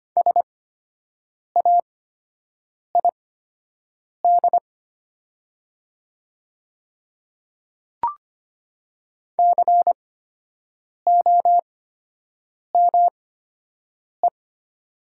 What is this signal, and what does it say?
Morse code, heard as a keyed tone, about 25 wpm, spaced as for 7 wpm, 700 Hz. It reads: SAID COME
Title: 8x Character Spacing - Top 200 Words - 25wpm